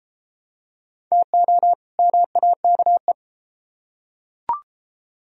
Morse code, heard as a keyed tone, about 33 wpm, 700 Hz. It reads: TO MAKE